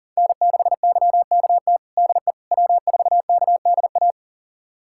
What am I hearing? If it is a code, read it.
Morse code, heard as a keyed tone, about 40 wpm, 700 Hz. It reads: N6YKT DE W4KDA
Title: USA Callsign DE Callsign - 5 Characters - 40wpm